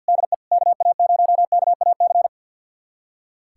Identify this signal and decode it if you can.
Morse code, heard as a keyed tone, about 46 wpm, 700 Hz. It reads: DE KA0KAC
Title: USA Callsign DE Callsign - 6 Characters - 50wpm